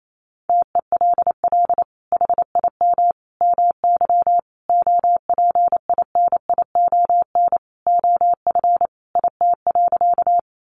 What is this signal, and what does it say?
Morse code, heard as a keyed tone, about 28 wpm, 700 Hz. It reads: TELL HIM MY OPINION OF IT.
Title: Sentences from Top 1200 Words 28wpm